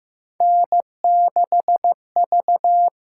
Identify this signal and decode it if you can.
Morse code, heard as a keyed tone, about 15 wpm, 700 Hz. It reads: N6V
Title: USA Call Signs - 6x Character Spacing - 15wpm